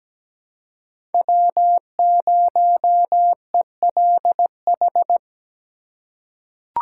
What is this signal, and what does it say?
Morse code, heard as a keyed tone, about 17 wpm, 700 Hz. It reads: W0ELH
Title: USA Call Signs - 2x Character Spacing - 17wpm